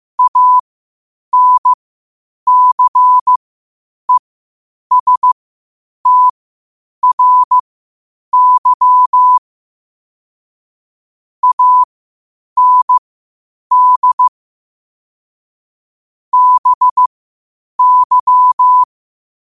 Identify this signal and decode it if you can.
Morse code, heard as a keyed tone, about 15 wpm, spaced as for 8 wpm, 1000 Hz. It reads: ANCESTRY AND BY
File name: qotdfiles / 8wpm